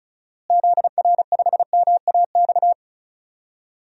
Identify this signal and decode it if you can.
Morse code, heard as a keyed tone, about 34 wpm, 700 Hz. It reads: ZR5MAX
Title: DXCC-100 Call Signs - 6 Characters - Repeated 3x - 35wpm